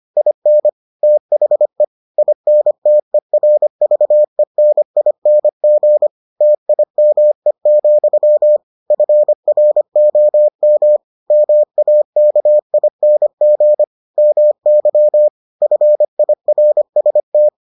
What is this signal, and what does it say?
Morse code, heard as a keyed tone, about 25 wpm, 600 Hz. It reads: IN THE INTERVENING TIME, FROM MAKING MY FIRST